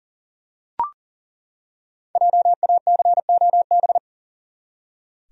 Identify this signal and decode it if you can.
Morse code, heard as a keyed tone, about 40 wpm, 700 Hz. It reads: JACOB
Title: USA Common Names - Max 5 Letters - 40wpm